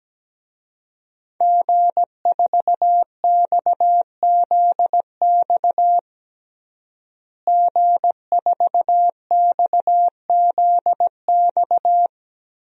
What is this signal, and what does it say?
Morse code, heard as a keyed tone, about 17 wpm, 700 Hz. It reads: G4XZX G4XZX
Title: DXCC-100 Call Signs - 5 Characters - Repeated 3x - 17wpm